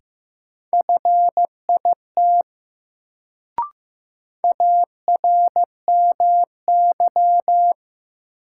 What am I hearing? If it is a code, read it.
Morse code, heard as a keyed tone, about 15 wpm, 700 Hz. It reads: FIT ARMY
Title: Sentences from Top 800 Words - New Word Drill - 15wpm